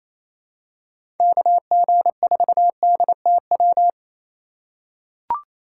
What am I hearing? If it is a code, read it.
Morse code, heard as a keyed tone, about 28 wpm, 700 Hz. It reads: KG4DTW
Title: USA Call Signs - 6 Characters - 2x Character Spacing - 28wpm